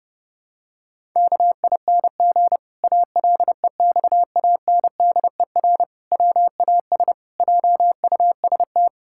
Morse code, heard as a keyed tone, about 32 wpm, 700 Hz. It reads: KING ALEXANDER WAS JUST
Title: Pangram Challenge v2 - 30wpm